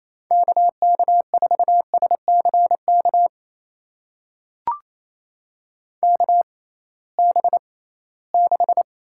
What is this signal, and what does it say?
Morse code, heard as a keyed tone, about 28 wpm, spaced as for 18 wpm, 700 Hz. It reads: KK4SCK KB6
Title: USA Call Signs - 6 Characters - 6x Character Spacing - 28wpm